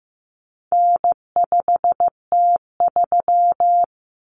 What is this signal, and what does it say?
Morse code, heard as a keyed tone, about 15 wpm, 700 Hz. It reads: N5T3
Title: Common Cut Numbers - 4-Digit 15wpm